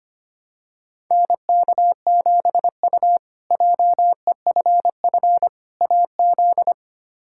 Morse code, heard as a keyed tone, about 25 wpm, 700 Hz. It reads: NK7U JEFF AZ
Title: K1USN SST QSO Practice - 25wpm